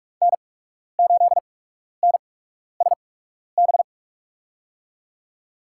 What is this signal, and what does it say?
Morse code, heard as a keyed tone, about 47 wpm, spaced as for 12 wpm, 690 Hz. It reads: N8NSB
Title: USA Call Signs - 5 Characters - 8x Character Spacing - 45wpm